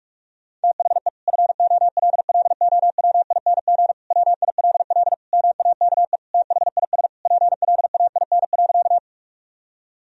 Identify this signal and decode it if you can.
Morse code, heard as a keyed tone, about 46 wpm, 700 Hz. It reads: THE FOLLOWING WILL MAKE THIS PLAIN.
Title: Sentences from Top 1000 Words - Review Part 4 - 45wpm